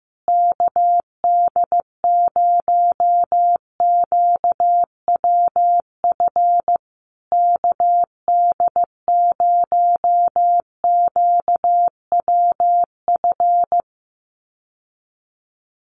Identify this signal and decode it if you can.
Morse code, heard as a keyed tone, about 15 wpm, 700 Hz. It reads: KD0QWF KD0QWF
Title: USA Call Signs Repeated 3x 15wpm